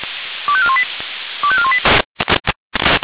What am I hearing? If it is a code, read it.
Five-tone selective call (ZVEI), heard as quick runs of five tones, tones of about 70 ms. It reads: 25418 25418